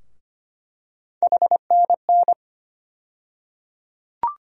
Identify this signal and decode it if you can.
Morse code, heard as a keyed tone, about 25 wpm, 700 Hz. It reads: HNN
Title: USA Call Sign Postfixes - 3 Letters 25wpm